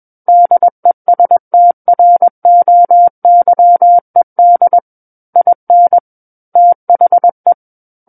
Morse code, heard as a keyed tone, about 21 wpm, 700 Hz. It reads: DESTROYED IN THE
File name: Book One / 12-Chapter 12 - 21 WPM